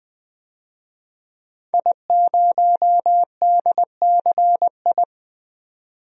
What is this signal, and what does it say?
Morse code, heard as a keyed tone, about 20 wpm, 700 Hz. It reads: I0DCI
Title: DXCC-100 Call Signs - 5 Characters - 20wpm